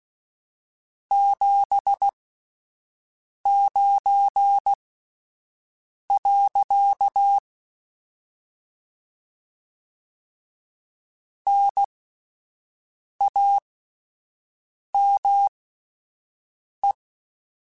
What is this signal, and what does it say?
Morse code, heard as a keyed tone, about 16 wpm, spaced as for 5 wpm, 800 Hz. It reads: 79. NAME
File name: QSOs Spaced